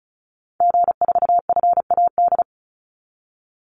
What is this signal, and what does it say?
Morse code, heard as a keyed tone, about 34 wpm, 700 Hz. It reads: G4FAD
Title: Call Signs - 35wpm